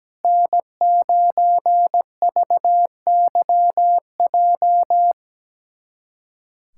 Morse code, heard as a keyed tone, about 17 wpm, 700 Hz. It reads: N9VYJ
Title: USA Callsign DE Callsign - 5 Characters - 17wpm